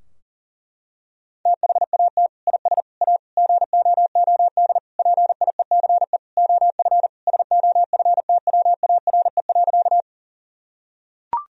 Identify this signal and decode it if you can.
Morse code, heard as a keyed tone, about 40 wpm, 700 Hz. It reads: THAT IS A GOOD PIECE OF SOFTWARE.
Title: Sentences from Top 1000 Words - Review Part 3 - 40wpm